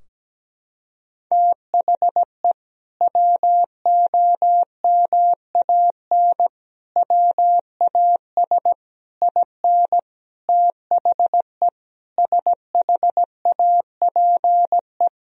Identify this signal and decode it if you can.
Morse code, heard as a keyed tone, about 17 wpm, 700 Hz. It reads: THE WOMAN WAS IN THE SHAPE
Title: Sentences from Top 1000 Words - Review Part 3 - 17wpm